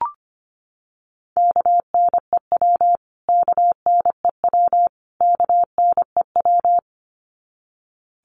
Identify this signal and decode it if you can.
Morse code, heard as a keyed tone, about 25 wpm, 700 Hz. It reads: KNEW KNEW KNEW
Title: Top 500 Words Repeated 25wpm